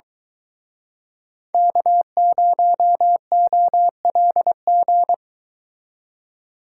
Morse code, heard as a keyed tone, about 24 wpm, 700 Hz. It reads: K0OLG